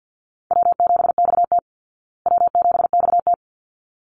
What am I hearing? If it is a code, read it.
Morse code, heard as a keyed tone, about 50 wpm, 700 Hz. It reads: W7XT W7XT
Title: Easy Call Signs Repeated 2x - 50wpm